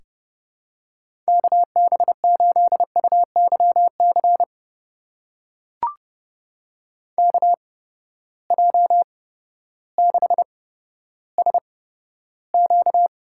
Morse code, heard as a keyed tone, about 30 wpm, spaced as for 14 wpm, 700 Hz. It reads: KB8UYC KJ6SQ
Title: USA Call Signs - 6 Characters - 8x Character Spacing - 30wpm